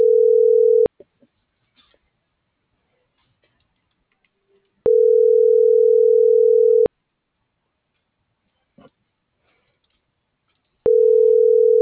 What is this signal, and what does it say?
Ringback tone (North America), 440 + 480 Hz.